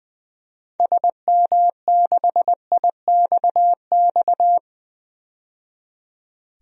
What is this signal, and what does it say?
Morse code, heard as a keyed tone, about 21 wpm, 700 Hz. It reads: SM6IXX
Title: DXCC-100 Call Signs - 6 Characters - 20wpm